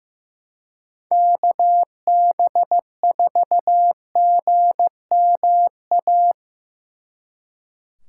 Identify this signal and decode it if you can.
Morse code, heard as a keyed tone, about 18 wpm, 700 Hz. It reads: KB4GMA
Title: Speed Racing - USA 2x3 Call Signs 15wpm